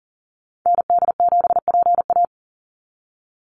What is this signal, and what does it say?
Morse code, heard as a keyed tone, about 40 wpm, 700 Hz. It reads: ND7PA
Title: NA SOTA Activator Callsigns 40wpm